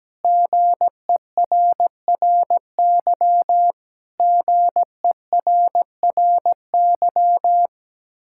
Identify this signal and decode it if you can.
Morse code, heard as a keyed tone, about 17 wpm, 700 Hz. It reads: GERRY GERRY
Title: Practice QSO - Exchange 1 - Beginning - 17wpm